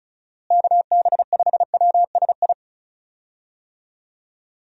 Morse code, heard as a keyed tone, about 36 wpm, 700 Hz. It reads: KB5WSI
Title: USA Call Signs - 6 Characters - Repeated 3x - 35wpm